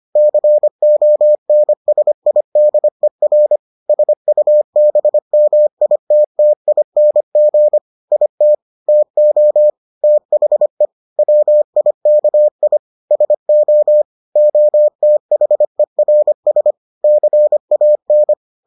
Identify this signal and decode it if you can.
Morse code, heard as a keyed tone, about 25 wpm, 600 Hz. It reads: CONSIDER SUBMITTING IT TO THE WIKI SO OTHERS CAN